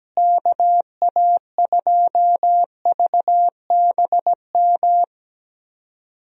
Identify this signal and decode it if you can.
Morse code, heard as a keyed tone, about 17 wpm, 700 Hz. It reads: KA2VBM
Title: USA Call Signs - 6 Characters - Repeated 2x - 17wpm